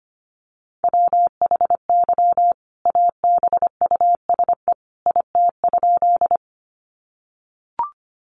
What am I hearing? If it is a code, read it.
Morse code, heard as a keyed tone, about 25 wpm, 700 Hz. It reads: WHY ABUSE IT?
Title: Sentences from Top 1300 Words 25wpm